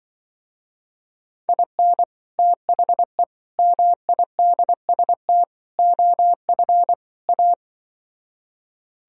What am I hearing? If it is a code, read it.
Morse code, heard as a keyed tone, about 24 wpm, 700 Hz. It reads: IN THE MIDST OF A
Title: Speed-Racing - Sets of 5 Words 20wpm